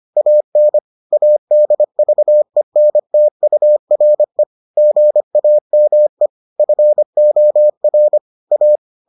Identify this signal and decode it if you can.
Morse code, heard as a keyed tone, about 25 wpm, 600 Hz. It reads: AN ADVENTURE GAME FOR A